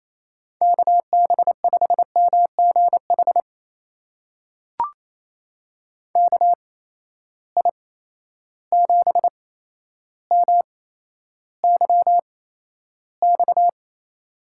Morse code, heard as a keyed tone, about 28 wpm, spaced as for 12 wpm, 700 Hz. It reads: KB5MGH KI7MYX
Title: USA Call Signs - 6 Characters - 8x Character Spacing - 28wpm